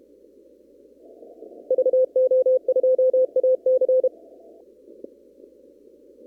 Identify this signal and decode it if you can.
Morse code, heard as a keyed tone, about 32 wpm, 520 Hz. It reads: VO2AC
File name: CW / kt5j wpx cw-op-k5pi